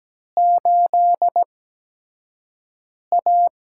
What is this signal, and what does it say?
Morse code, heard as a keyed tone, about 17 wpm, spaced as for 5 wpm, 700 Hz. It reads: 8A
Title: USA Call Signs - 4 Characters - 8x Character Spacing - 17wpm